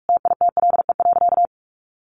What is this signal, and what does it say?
Morse code, heard as a keyed tone, about 44 wpm, 700 Hz. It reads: TITLE.
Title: Sentences from Top 1200 Words 45wpm